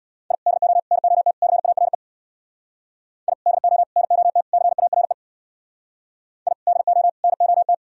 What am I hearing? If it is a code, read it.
Morse code, heard as a keyed tone, about 61 wpm, 700 Hz. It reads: I DO NOT CARE I DO NOT CARE I DO NOT
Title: Speed-Racing - Sets of 4 Words 50wpm